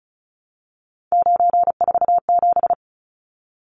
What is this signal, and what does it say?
Morse code, heard as a keyed tone, about 34 wpm, 700 Hz. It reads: 947
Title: Numbers - 3-Digit 35wpm